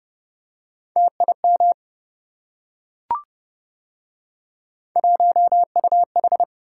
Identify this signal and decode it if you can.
Morse code, heard as a keyed tone, about 28 wpm, 700 Hz. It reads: TIM 1UH